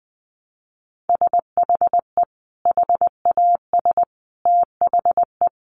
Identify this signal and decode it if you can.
Morse code, heard as a keyed tone, about 20 wpm, 700 Hz. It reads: SHE HAS THE
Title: Sentences from Top 800 Words 20wpm